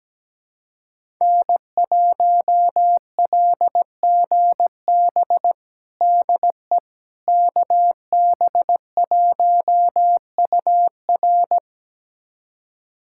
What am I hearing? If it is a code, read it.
Morse code, heard as a keyed tone, about 17 wpm, 700 Hz. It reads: N1LGB DE KB1UR
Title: USA Callsign DE Callsign - 5 Characters - 17wpm